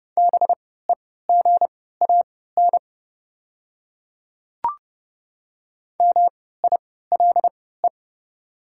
Morse code, heard as a keyed tone, about 30 wpm, spaced as for 17 wpm, 700 Hz. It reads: BEGAN MILE
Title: 3x Character Spacing - Top 300 Words - 30wpm